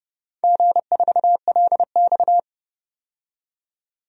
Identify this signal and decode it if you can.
Morse code, heard as a keyed tone, about 30 wpm, 700 Hz. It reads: G4LX
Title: DXCC-100 Call Signs - 4 Characters - 30wpm